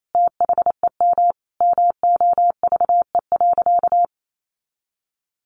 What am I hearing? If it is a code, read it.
Morse code, heard as a keyed tone, about 28 wpm, 700 Hz. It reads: THEM MOVE.
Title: A Sound Of Thunder 28wpm